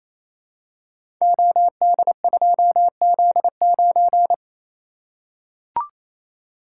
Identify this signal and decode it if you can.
Morse code, heard as a keyed tone, about 27 wpm, 700 Hz. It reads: OD2Z9
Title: Code Groups - 5 Letter:Numbers - 28wpm